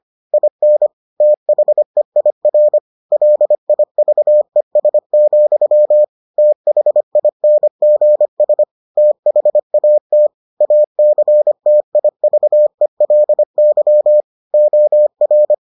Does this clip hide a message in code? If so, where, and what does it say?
Yes: Morse code from 0.3 to 15.5 s, heard as a keyed tone, about 25 wpm, 600 Hz. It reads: IN THEIR LIVES, THINGS THAT ACTIVELY OR